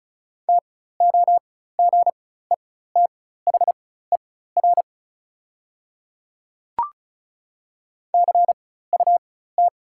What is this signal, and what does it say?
Morse code, heard as a keyed tone, about 36 wpm, spaced as for 16 wpm, 690 Hz. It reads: TOGETHER CUT
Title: 4x Character Spacing - Top 300 Words - 35wpm